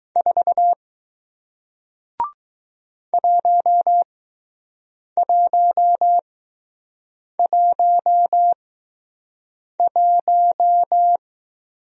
Morse code, heard as a keyed tone, about 20 wpm, spaced as for 6 wpm, 700 Hz. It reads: 4 1111